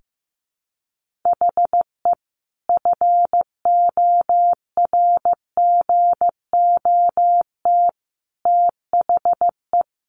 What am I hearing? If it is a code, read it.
Morse code, heard as a keyed tone, about 15 wpm, 700 Hz. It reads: HE FORGOT THE
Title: Sentences from Top 1400 Words 15wpm